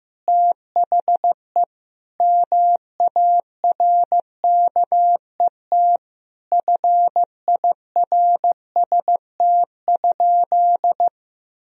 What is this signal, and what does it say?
Morse code, heard as a keyed tone, about 15 wpm, 700 Hz. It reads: THE MARKET FIRST?
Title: Sentences from Top 1000 Words - Review Part 3 - 15wpm